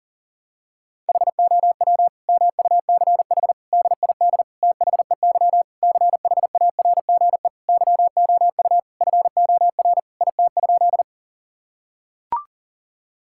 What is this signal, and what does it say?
Morse code, heard as a keyed tone, about 40 wpm, 700 Hz. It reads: HOW MUCH DID THEY CHARGE YOU FOR IT?
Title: Sentences from Top 1000 Words - Review Part 6 - 40wpm